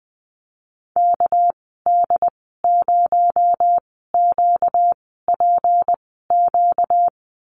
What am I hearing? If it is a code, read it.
Morse code, heard as a keyed tone, about 20 wpm, spaced as for 14 wpm, 700 Hz. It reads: KD0QPQ
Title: 245 Lesson 39g - Letter z - Call Signs 20wpm